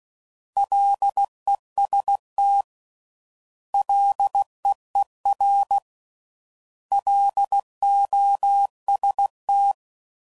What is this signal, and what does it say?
Morse code, heard as a keyed tone, about 16 wpm, 800 Hz. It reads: LEST LEER LOST